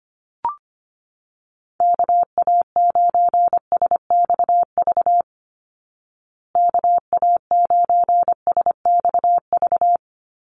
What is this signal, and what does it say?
Morse code, heard as a keyed tone, about 25 wpm, 700 Hz. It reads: KA9SXV KA9SXV
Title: Call Signs Repeated 3x - 25wpm